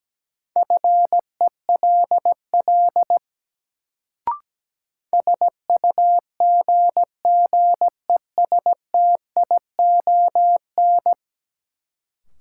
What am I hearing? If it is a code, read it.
Morse code, heard as a keyed tone, about 17 wpm, 700 Hz. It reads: FELL SUGGESTION
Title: Sentences from Top 1600 Words - New Word Drill Alt - 17wpm